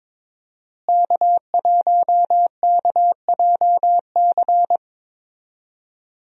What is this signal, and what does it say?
Morse code, heard as a keyed tone, about 22 wpm, 700 Hz. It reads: K1KJC
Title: USA Call Signs - 5 Characters - Repeated 3x - 22wpm